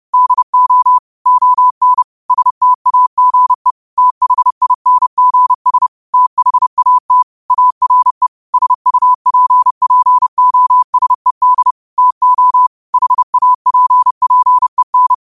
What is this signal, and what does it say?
Morse code, heard as a keyed tone, about 30 wpm, 1000 Hz. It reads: DO ON STAGE THINGS THAT ARE SUPPOSED TO HAPPEN
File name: qotdfiles / 30wpm